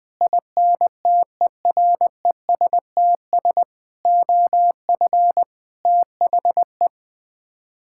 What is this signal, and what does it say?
Morse code, heard as a keyed tone, about 20 wpm, 700 Hz. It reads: INTERESTS OF THE
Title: Sets of 6 Words - 20wpm